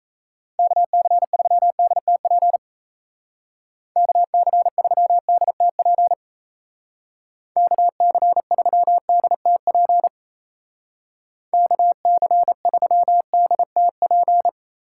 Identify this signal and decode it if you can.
Morse code, heard as a keyed tone, about 35 wpm, 700 Hz. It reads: KC3DTP KC3DTP KC3DTP KC3DTP